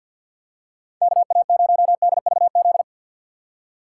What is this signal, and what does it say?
Morse code, heard as a keyed tone, about 52 wpm, 700 Hz. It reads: KA0DVZ
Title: USA Call Signs - 6 Characters - 6x Character Spacing - 50wpm